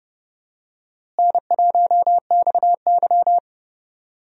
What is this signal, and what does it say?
Morse code, heard as a keyed tone, about 29 wpm, 700 Hz. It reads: N1XY
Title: USA Call Signs - 4 Characters - Repeated 3x - 30wpm